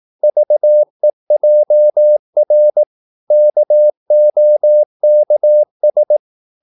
Morse code, heard as a keyed tone, about 18 wpm, 600 Hz. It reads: VEJR KOKS
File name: Korte ord DK 18wpm